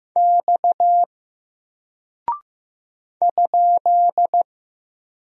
Morse code, heard as a keyed tone, about 15 wpm, 700 Hz. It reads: X ?
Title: Single Character (Letter:Number:Punctuation:Prosign) Alt 15wpm